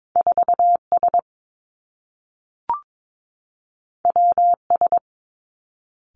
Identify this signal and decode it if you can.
Morse code, heard as a keyed tone, about 22 wpm, 700 Hz. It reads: 4 S W S
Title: Two Challenging Letter-Number Combinations 22wpm